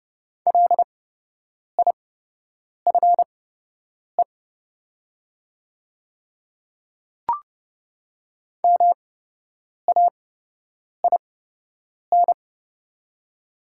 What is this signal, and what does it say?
Morse code, heard as a keyed tone, about 30 wpm, spaced as for 8 wpm, 700 Hz. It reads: LIFE MAIN
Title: 8x Character Spacing - Top 300 Words - 30wpm